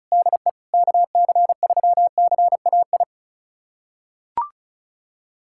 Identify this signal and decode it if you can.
Morse code, heard as a keyed tone, about 36 wpm, 700 Hz. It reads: DE KC3CAI
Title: USA Callsign DE Callsign - 6 Characters - 35wpm